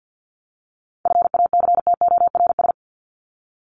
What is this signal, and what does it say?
Morse code, heard as a keyed tone, about 50 wpm, 700 Hz. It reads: FACTORS
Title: ES:S Suffix - Top 100 Words with Base Words 50wpm